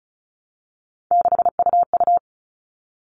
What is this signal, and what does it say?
Morse code, heard as a keyed tone, about 34 wpm, 700 Hz. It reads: 6UU
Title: All Cut Numbers - 3-Digit 35wpm